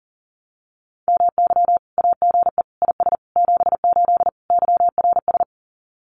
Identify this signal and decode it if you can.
Morse code, heard as a keyed tone, about 37 wpm, 700 Hz. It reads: MY AGE IS 78 YRS
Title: Common QSO Phrases Encore 40wpm